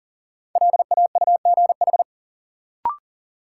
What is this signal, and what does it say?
Morse code, heard as a keyed tone, about 40 wpm, 700 Hz. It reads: LAUGH
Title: Sentences from Top 500 Words - New Word Drill - 40wpm